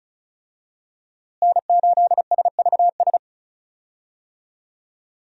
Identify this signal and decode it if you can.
Morse code, heard as a keyed tone, about 34 wpm, 700 Hz. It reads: N8SVS
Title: USA Call Signs - 5 Characters - Repeated 2x - 35wpm